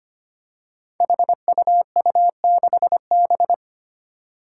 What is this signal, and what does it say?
Morse code, heard as a keyed tone, about 25 wpm, 700 Hz. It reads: HUU6B
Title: Code Groups - 5 Letter:Numbers - 25wpm